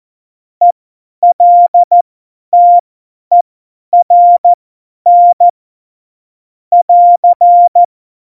Morse code, heard as a keyed tone, about 14 wpm, spaced as for 10 wpm, 700 Hz. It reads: ELTERN <AR>
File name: Morsekurs / Witze / Witz 27. Februar 2018, 50 F70